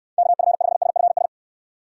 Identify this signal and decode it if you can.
Morse code, heard as a keyed tone, about 62 wpm, 700 Hz. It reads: KF5SFS